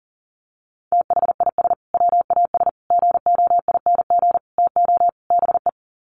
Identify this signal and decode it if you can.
Morse code, heard as a keyed tone, about 40 wpm, 700 Hz. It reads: THIS WAS GOING TO BE
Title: Sets of 5 Words Encore2 40wpm